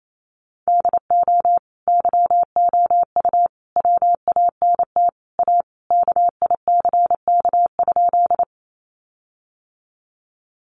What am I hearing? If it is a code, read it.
Morse code, heard as a keyed tone, about 28 wpm, 700 Hz. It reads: DO YOU WANT A KICK?
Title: Sentences from Top 1300 Words 28wpm